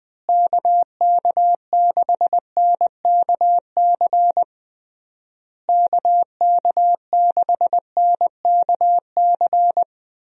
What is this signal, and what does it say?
Morse code, heard as a keyed tone, about 19 wpm, 700 Hz. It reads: KK6NKC KK6NKC
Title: USA Call Signs - 6 Characters - Repeated 3x - 20wpm